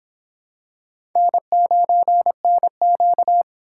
Morse code, heard as a keyed tone, about 25 wpm, 700 Hz. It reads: N9NQ